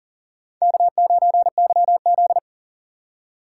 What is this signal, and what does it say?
Morse code, heard as a keyed tone, about 40 wpm, 700 Hz. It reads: K9YZ
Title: USA Call Signs - 4 Characters - 4x Character Spacing - 40wpm